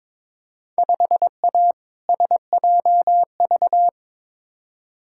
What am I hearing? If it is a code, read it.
Morse code, heard as a keyed tone, about 22 wpm, 700 Hz. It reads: 5A SJV
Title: Field Day Exchanges - 22wpm